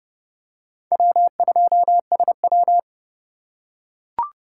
Morse code, heard as a keyed tone, about 32 wpm, 700 Hz. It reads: W2SW
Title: USA Call Signs - 4 Characters - 4x Character Spacing - 30wpm